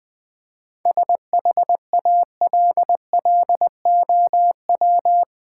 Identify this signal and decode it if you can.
Morse code, heard as a keyed tone, about 20 wpm, 700 Hz. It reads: SHALLOW
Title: 7 Letter Words - First 3 Letters Given - 20wpm